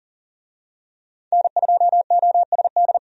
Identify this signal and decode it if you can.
Morse code, heard as a keyed tone, about 40 wpm, 700 Hz. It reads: N2OSD
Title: USA Call Signs - 5 Characters - 40wpm